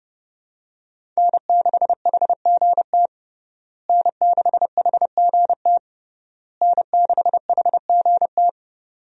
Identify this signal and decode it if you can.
Morse code, heard as a keyed tone, about 29 wpm, 700 Hz. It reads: N6HGT N6HGT N6HGT
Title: USA Call Signs - 5 Characters - Repeated 3x - 30wpm